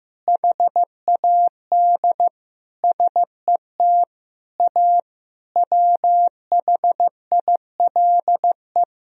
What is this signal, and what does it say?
Morse code, heard as a keyed tone, about 15 wpm, 700 Hz. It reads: HAD SET A WHILE
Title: Sentences from Top 1000 Words - Review Part 1 - 15wpm